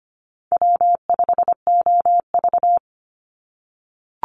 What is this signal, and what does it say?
Morse code, heard as a keyed tone, about 25 wpm, 700 Hz. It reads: W5OV
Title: Call Signs - 25wpm